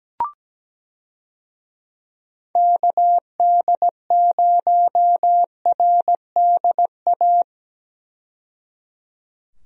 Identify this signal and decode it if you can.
Morse code, heard as a keyed tone, about 17 wpm, 700 Hz. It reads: KD0RDA
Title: USA Call Signs - 6 Characters - 17wpm